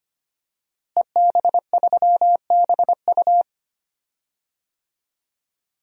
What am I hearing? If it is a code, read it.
Morse code, heard as a keyed tone, about 25 wpm, 700 Hz. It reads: EB3BU
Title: DXCC-100 Call Signs - 5 Characters - Repeated 2x - 25wpm